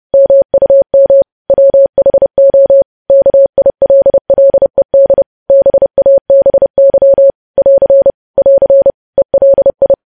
Morse code, heard as a keyed tone, about 30 wpm, 550 Hz. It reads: MUM WHO KILLED BABY <AR> <AR> ELI
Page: News Headlines in Morse Code at 30 WPM